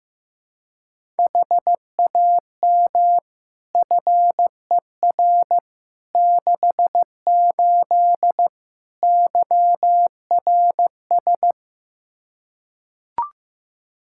Morse code, heard as a keyed tone, about 15 wpm, 700 Hz. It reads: HAM FER 68 YRS
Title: K1USN SST QSO Practice 15wpm